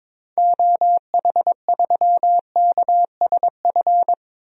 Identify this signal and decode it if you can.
Morse code, heard as a keyed tone, about 22 wpm, 700 Hz. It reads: OH3KSF
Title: DXCC-100 Call Signs - 6 Characters - Repeated 2x - 22wpm